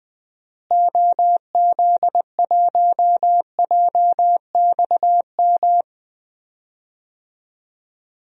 Morse code, heard as a keyed tone, about 20 wpm, 700 Hz. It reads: OZ1JXM
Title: DXCC-100 Call Signs - 6 Characters - Repeated 2x - 20wpm